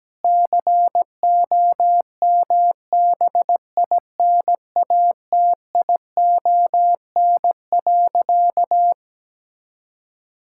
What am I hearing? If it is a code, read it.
Morse code, heard as a keyed tone, about 17 wpm, 700 Hz. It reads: COMBINATION.
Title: Sentences from Top 1000 Words - Review Part 6 - 17wpm